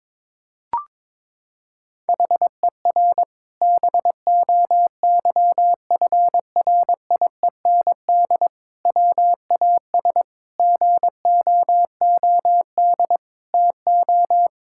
Morse code, heard as a keyed tone, about 22 wpm, 700 Hz. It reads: HER BOYFRIEND WAS GOOD TO
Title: Sentences from Top 1000 Words - Review Part 5 - 22wpm